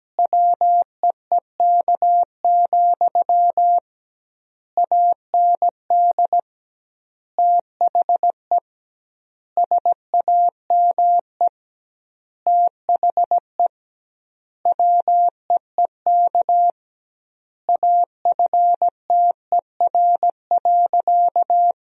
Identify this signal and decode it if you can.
Morse code, heard as a keyed tone, about 17 wpm, 700 Hz. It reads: WEEK, AND THE SAME THE WEEK AFTER.
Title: 2x Word-Spacing - Sentences from Top 500 Words - 17wpm